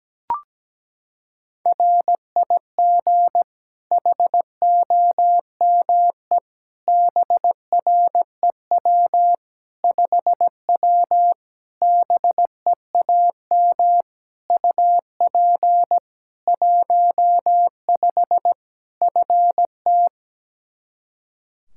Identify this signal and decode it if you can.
Morse code, heard as a keyed tone, about 17 wpm, 700 Hz. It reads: RIG HOME BREW 5W BEAM UP 15 FT
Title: K1USN SST QSO Practice 17wpm